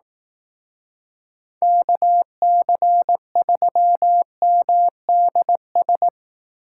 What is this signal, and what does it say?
Morse code, heard as a keyed tone, about 18 wpm, 700 Hz. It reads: KC3MDS
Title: Speed Racing - USA 2x3 Call Signs 15wpm